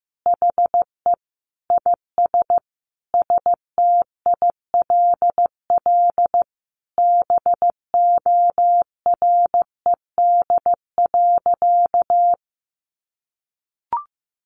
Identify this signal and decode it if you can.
Morse code, heard as a keyed tone, about 15 wpm, 700 Hz. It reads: HE IS STILL BORED.
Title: Easy Phrases 15wpm